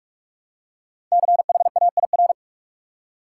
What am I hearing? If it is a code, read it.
Morse code, heard as a keyed tone, about 46 wpm, 700 Hz. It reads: CHAIR
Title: Five Letter Words - 45wpm